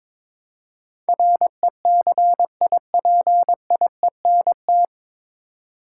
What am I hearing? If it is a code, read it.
Morse code, heard as a keyed tone, about 22 wpm, 690 Hz. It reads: RECIPIENT
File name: 9 Letter Words - First 3 Letters Given - 22wpm